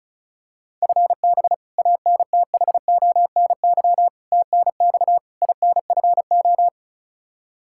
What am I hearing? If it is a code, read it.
Morse code, heard as a keyed tone, about 34 wpm, 700 Hz. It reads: FB ANTHONY TNX INFO
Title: Practice QSO - Exchange 3 - More Info - 35wpm